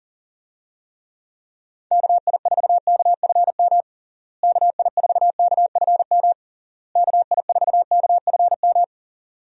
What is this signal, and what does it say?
Morse code, heard as a keyed tone, about 40 wpm, 700 Hz. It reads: KI4KFM KI4KFM KI4KFM
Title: USA Call Signs - 6 Characters - Repeated 3x - 40wpm